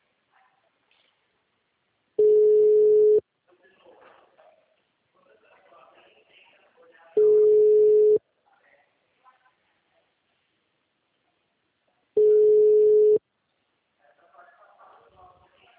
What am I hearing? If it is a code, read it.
Ringback tone (Europe, 425 Hz), with speech.
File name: out- -2011- - - .70571